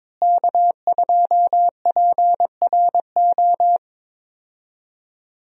Morse code, heard as a keyed tone, about 21 wpm, 700 Hz. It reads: K2PRO
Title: USA Call Signs - 5 Characters - Repeated 3x - 22wpm